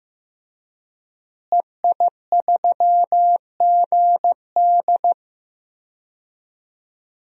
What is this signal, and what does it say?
Morse code, heard as a keyed tone, about 16 wpm, 700 Hz. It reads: EI3GD
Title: DXCC-100 Call Signs - 5 Characters - Repeated 2x - 15wpm